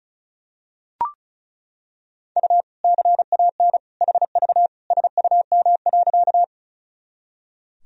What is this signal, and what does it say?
Morse code, heard as a keyed tone, about 36 wpm, 700 Hz. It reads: U CAN HV SUM.
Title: Sentences from Top 100 Words and Abbreviations 35wpm